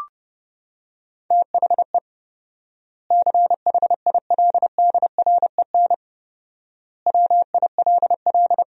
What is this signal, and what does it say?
Morse code, heard as a keyed tone, about 30 wpm, 700 Hz. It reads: THE CHILDREN WILL
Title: 4x Word-Spacing - Sentences from Top 300 Words - 30wpm